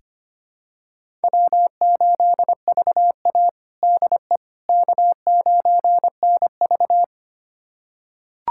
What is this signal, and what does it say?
Morse code, heard as a keyed tone, about 25 wpm, 700 Hz. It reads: W8VA DE K9NV
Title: USA Callsign DE Callsign - 4 Characters - 25wpm